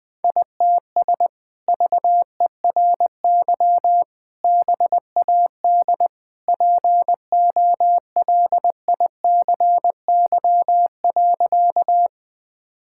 Morse code, heard as a keyed tone, about 20 wpm, 700 Hz. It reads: ITS VERY BAD POLICY.
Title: Sentences from Top 1000 Words - Review Part 5 - 20wpm